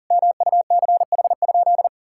Morse code, heard as a keyed tone, about 40 wpm, 700 Hz. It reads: MUCH?
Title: Sentences from Top 1000 Words - Review Part 6 - 40wpm